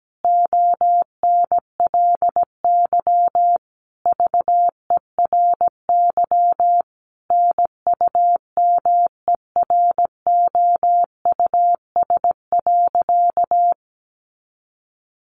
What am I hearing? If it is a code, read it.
Morse code, heard as a keyed tone, about 17 wpm, 700 Hz. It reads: ONLY VERY NUMEROUS.
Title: Sentences from Top 1400 Words 17wpm